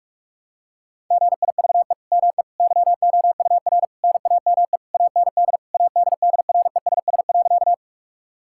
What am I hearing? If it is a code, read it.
Morse code, heard as a keyed tone, about 41 wpm, 700 Hz. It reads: GIVE ME YOUR NAME AND ADDRESS.
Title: Sentences from Top 1000 Words - Review Part 6 - 45wpm